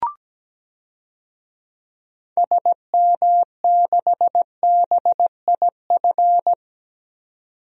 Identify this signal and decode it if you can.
Morse code, heard as a keyed tone, about 17 wpm, 700 Hz. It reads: SM6BIF
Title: DXCC-100 Call Signs - 6 Characters - 17wpm